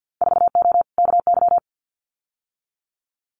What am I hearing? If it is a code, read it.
Morse code, heard as a keyed tone, about 50 wpm, 700 Hz. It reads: 4O KY
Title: Winter Field Day Exchanges Encore 50wpm